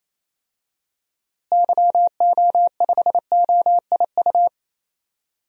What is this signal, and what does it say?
Morse code, heard as a keyed tone, about 28 wpm, 700 Hz. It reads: YO5OIU
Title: DXCC-100 Call Signs - 6 Characters - Repeated 3x - 28wpm